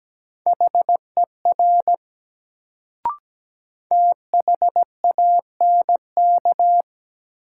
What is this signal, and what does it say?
Morse code, heard as a keyed tone, about 17 wpm, 700 Hz. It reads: HER THANK
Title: Sets of 2 Words - New Word Drill - 17wpm